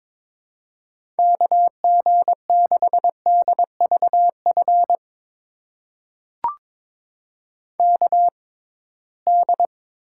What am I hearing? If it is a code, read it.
Morse code, heard as a keyed tone, about 22 wpm, spaced as for 17 wpm, 700 Hz. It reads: KG6DVF KD
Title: USA Call Signs - 6 Characters - 6x Character Spacing - 22wpm